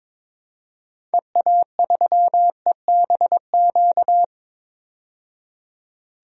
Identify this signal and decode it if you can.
Morse code, heard as a keyed tone, about 22 wpm, 700 Hz. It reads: EA3EBQ
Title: DXCC-100 Call Signs - 6 Characters - 22wpm